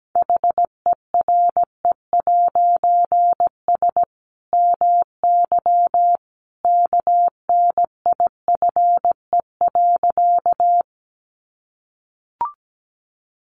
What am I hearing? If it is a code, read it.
Morse code, heard as a keyed tone, about 17 wpm, 700 Hz. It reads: HERE'S MY KNIFE.
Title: A Sound Of Thunder 17wpm